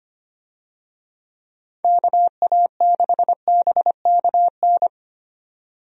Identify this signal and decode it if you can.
Morse code, heard as a keyed tone, about 25 wpm, 700 Hz. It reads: KA6BKN
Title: USA Call Signs - 6 Characters - 25wpm